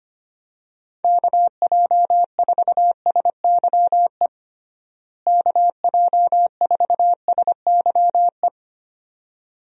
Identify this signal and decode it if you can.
Morse code, heard as a keyed tone, about 25 wpm, 700 Hz. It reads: KJ4SYE KJ4SYE
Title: USA Call Signs - 6 Characters - Repeated 2x - 25wpm